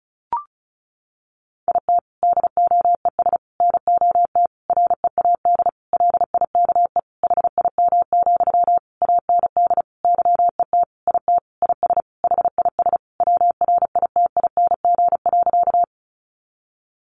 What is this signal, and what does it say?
Morse code, heard as a keyed tone, about 34 wpm, 700 Hz. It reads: IT DOES NOT READ LIKE HIM, AND YET IT IS HIS WRITING.
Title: Sentences from Top 800 Words 35wpm